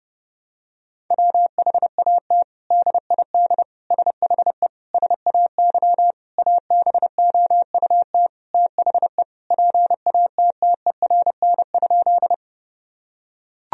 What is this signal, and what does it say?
Morse code, heard as a keyed tone, about 32 wpm, 700 Hz. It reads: WHAT DID SHE SAY ABOUT THE PATTERN?
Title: Sentences from Top 1000 Words - Review Part 2 - 30wpm